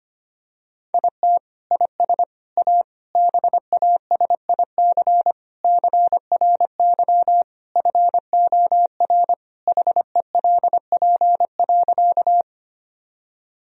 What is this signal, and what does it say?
Morse code, heard as a keyed tone, about 25 wpm, 700 Hz. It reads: IT IS A BASIC CRY FOR HELP.
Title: Sentences from Top 1000 Words - Review Part 3 - 25wpm